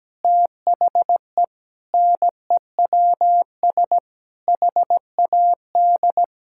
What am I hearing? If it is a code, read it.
Morse code, heard as a keyed tone, about 17 wpm, 700 Hz. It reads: THE NEWS HAD
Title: Sentences from Top 1600 Words 17wpm